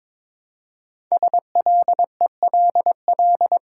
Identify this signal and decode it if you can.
Morse code, heard as a keyed tone, about 22 wpm, 700 Hz. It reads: SLELL
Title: Code Groups - 5 Letters - 22wpm